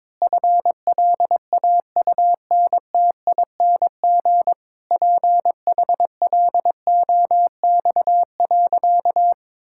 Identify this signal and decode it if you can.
Morse code, heard as a keyed tone, about 22 wpm, 700 Hz. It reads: FLAUNTING PHLOX.
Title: Pangram Challenge v2 - 22wpm